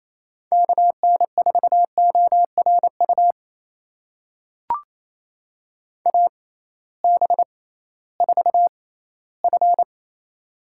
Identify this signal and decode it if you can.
Morse code, heard as a keyed tone, about 28 wpm, spaced as for 16 wpm, 700 Hz. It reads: KN4ORU AB4F
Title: USA Call Signs - 6x Character Spacing - 28wpm